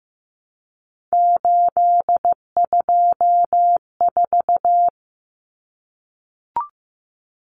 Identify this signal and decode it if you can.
Morse code, heard as a keyed tone, about 15 wpm, 700 Hz. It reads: 824
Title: Common Cut Numbers - 3-Digit 15wpm